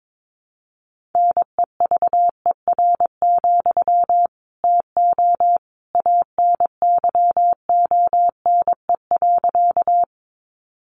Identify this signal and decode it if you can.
Morse code, heard as a keyed tone, about 22 wpm, 700 Hz. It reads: NEVER, TO ANYONE.
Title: Sentences from Top 900 Words 22wpm